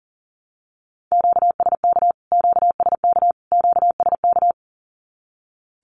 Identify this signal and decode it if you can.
Morse code, heard as a keyed tone, about 40 wpm, 700 Hz. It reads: QSK QSK QSK
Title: Q-Codes Repeated 3x 40wpm